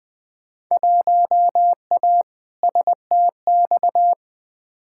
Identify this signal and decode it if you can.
Morse code, heard as a keyed tone, about 20 wpm, 700 Hz. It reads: 1A STX
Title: Field Day Exchanges - 20wpm